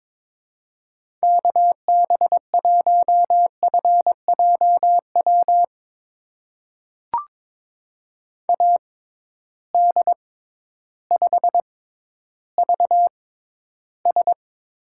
Morse code, heard as a keyed tone, about 23 wpm, spaced as for 12 wpm, 700 Hz. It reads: KB1FJW AD5VS
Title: USA Call Signs - 6x Character Spacing - 22wpm